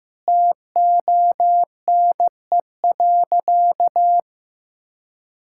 Morse code, heard as a keyed tone, about 15 wpm, 700 Hz. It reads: TONE.
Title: Sentences from Top 1600 Words 15wpm